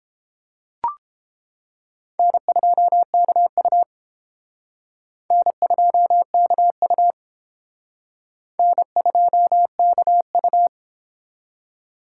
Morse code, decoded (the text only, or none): N2KU N2KU N2KU